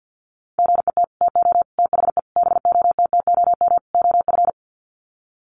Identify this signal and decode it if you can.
Morse code, heard as a keyed tone, about 50 wpm, 700 Hz. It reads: GET TO THE BOTTOM OF
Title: Sets of 5 Words Encore 50wpm